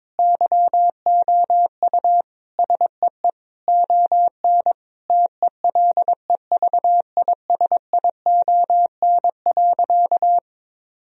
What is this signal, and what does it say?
Morse code, heard as a keyed tone, about 22 wpm, 700 Hz. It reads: YOU SEE ON TELEVISION.
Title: Sentences from Top 1000 Words - Review Part 4 - 22wpm